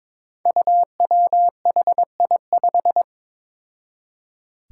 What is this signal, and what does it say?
Morse code, heard as a keyed tone, about 22 wpm, 700 Hz. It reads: UWHI5
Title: Five Challenging Letter-Number Combinations 22wpm